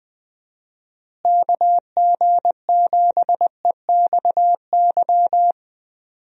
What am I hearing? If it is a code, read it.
Morse code, heard as a keyed tone, about 21 wpm, 700 Hz. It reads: KG7EXY
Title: USA Call Signs - 6 Characters - Repeated 2x - 20wpm